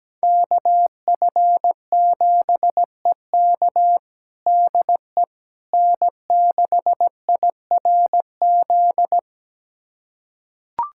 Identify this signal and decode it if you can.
Morse code, heard as a keyed tone, about 17 wpm, 700 Hz. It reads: KF7EK DE N6IRZ
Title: USA Callsign DE Callsign - 5 Characters - 17wpm